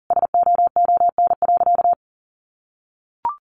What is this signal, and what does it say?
Morse code, heard as a keyed tone, about 40 wpm, 700 Hz. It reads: SOON.
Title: Sentences from Top 500 Words 40wpm